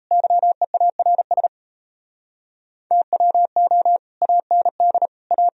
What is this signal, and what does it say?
Morse code, heard as a keyed tone, about 38 wpm, 700 Hz. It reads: YEARS TWO AND A